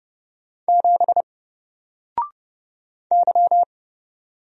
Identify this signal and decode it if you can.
Morse code, heard as a keyed tone, about 30 wpm, 700 Hz. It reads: 7 Y
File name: Single-Letter-Number-30wpm